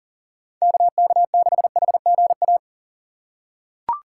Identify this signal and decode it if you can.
Morse code, heard as a keyed tone, about 40 wpm, 700 Hz. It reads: KK6HGA
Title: USA Call Signs - 6 Characters - 3x Character Spacing - 40wpm